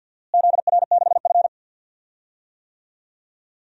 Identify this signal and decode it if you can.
Morse code, heard as a keyed tone, about 51 wpm, 700 Hz. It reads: ZR6F
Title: DXCC-100 Call Signs - 4 Characters - Repeated 2x - 50wpm